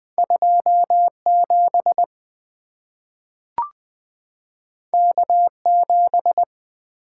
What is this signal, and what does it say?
Morse code, heard as a keyed tone, about 20 wpm, 700 Hz. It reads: 27 K7
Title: Two Letter-Number Combinations - 20wpm